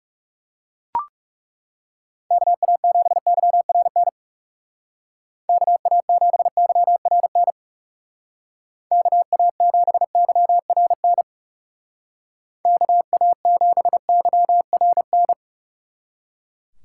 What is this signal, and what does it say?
Morse code, heard as a keyed tone, about 38 wpm, 700 Hz. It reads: KA7YRN KA7YRN KA7YRN KA7YRN